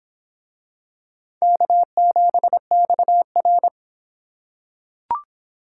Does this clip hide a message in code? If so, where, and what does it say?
Yes: Morse code from 1.4 to 3.7 s, heard as a keyed tone, about 24 wpm, 700 Hz. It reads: K7XR